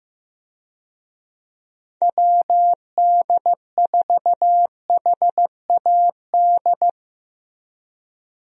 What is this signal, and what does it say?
Morse code, heard as a keyed tone, about 15 wpm, 700 Hz. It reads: WD4HAD
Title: USA Call Signs - 6 Characters - Repeated 3x - 15wpm